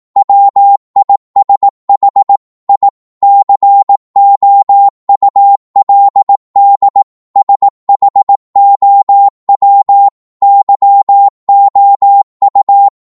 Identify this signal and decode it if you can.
Morse code, heard as a keyed tone, about 18 wpm, 800 Hz. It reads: WISH I COULD SHOW YOU
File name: Morse / psg-18wpm